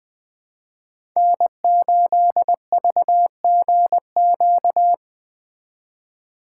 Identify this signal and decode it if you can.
Morse code, heard as a keyed tone, about 20 wpm, 700 Hz. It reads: N8VGQ
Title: Speed Racing - USA Call Signs 17wpm